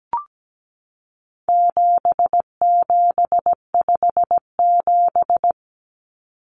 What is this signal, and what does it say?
Morse code, heard as a keyed tone, about 17 wpm, 700 Hz. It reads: 7757
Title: Numbers - 4-Digit 17wpm